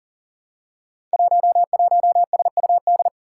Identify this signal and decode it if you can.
Morse code, heard as a keyed tone, about 42 wpm, 700 Hz. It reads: 11SUD
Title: Five Challenging Letter-Number Combinations 40wpm